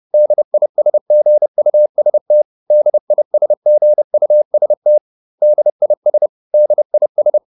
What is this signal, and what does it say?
Morse code, heard as a keyed tone, about 30 wpm, 600 Hz. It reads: DISGUST DISGUST DIS DIS